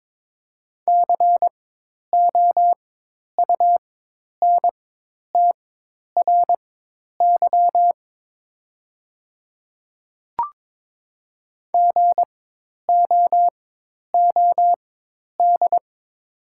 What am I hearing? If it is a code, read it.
Morse code, heard as a keyed tone, about 22 wpm, spaced as for 10 wpm, 700 Hz. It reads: COUNTRY GOOD
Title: 4x Character Spacing - Top 300 Words - 22wpm